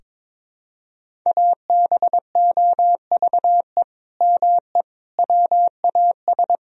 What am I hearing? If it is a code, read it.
Morse code, heard as a keyed tone, about 22 wpm, 700 Hz. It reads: ABOVE ME WAS
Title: Sentences from Top 1000 Words - Review Part 3 - 22wpm